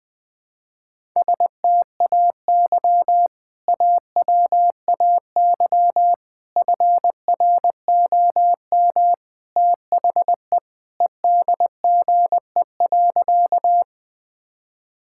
Morse code, heard as a keyed tone, about 21 wpm, 700 Hz. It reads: STAY AWAY FROM THE EDGE.
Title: Sentences with 3-Word Phrases - 20wpm